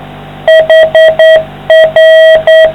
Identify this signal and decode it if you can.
Morse code, heard as a keyed tone, about 9 wpm, 660 Hz. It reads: HR